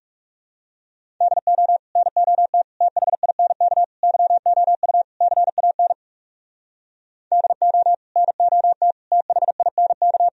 Morse code, heard as a keyed tone, about 37 wpm, 700 Hz. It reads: DO NOT THINK YOU CAN DO NOT THINK
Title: Speed-Racing - Sets of 5 Words 30wpm